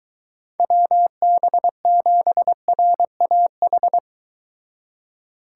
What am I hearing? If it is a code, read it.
Morse code, heard as a keyed tone, about 23 wpm, 700 Hz. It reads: WB7RAH